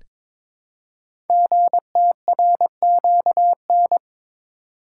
Morse code, heard as a keyed tone, about 22 wpm, 700 Hz. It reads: GTRQN
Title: Code Groups - 5 Letters - 22wpm